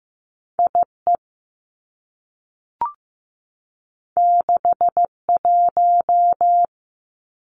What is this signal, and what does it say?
Morse code, heard as a keyed tone, about 15 wpm, 700 Hz. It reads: I E 6 1
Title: Two Challenging Letter-Number Combinations 15wpm